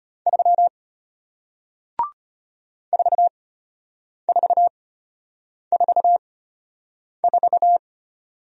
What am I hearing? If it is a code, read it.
Morse code, heard as a keyed tone, about 33 wpm, spaced as for 8 wpm, 700 Hz. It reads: 3 4444